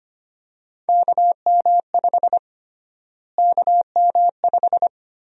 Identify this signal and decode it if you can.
Morse code, heard as a keyed tone, about 25 wpm, 700 Hz. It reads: KM5 KM5
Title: USA Call Sign Prefixes - Repeated 2x - 25wpm